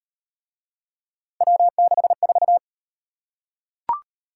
Morse code, heard as a keyed tone, about 33 wpm, 700 Hz. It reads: W64